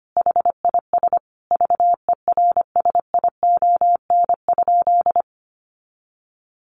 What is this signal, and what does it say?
Morse code, heard as a keyed tone, about 25 wpm, 700 Hz. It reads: HIS VERSION?
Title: Sentences from Top 1400 Words 25wpm